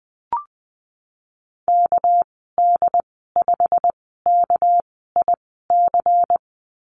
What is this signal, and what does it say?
Morse code, heard as a keyed tone, about 20 wpm, spaced as for 15 wpm, 700 Hz. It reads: KD5KIC
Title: 207 Lesson 33g - Letter j - Call Signs 20wpm